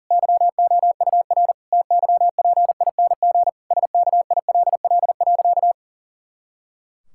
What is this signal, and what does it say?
Morse code, heard as a keyed tone, about 40 wpm, 700 Hz. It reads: YOUR TYPING SKILL.
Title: Pangram Challenge v2 - 40wpm